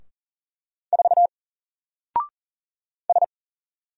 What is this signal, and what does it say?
Morse code, heard as a keyed tone, about 40 wpm, 700 Hz. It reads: 4 S
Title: Single Letter-Number 40wpm